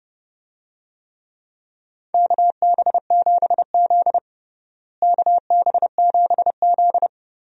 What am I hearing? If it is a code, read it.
Morse code, heard as a keyed tone, about 30 wpm, 700 Hz. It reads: KB7Z KB7Z
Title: USA Call Signs - 4 Characters - Repeated 2x - 30wpm